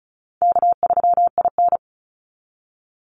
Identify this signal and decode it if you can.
Morse code, heard as a keyed tone, about 34 wpm, 700 Hz. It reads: K3IN
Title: Call Signs - 35wpm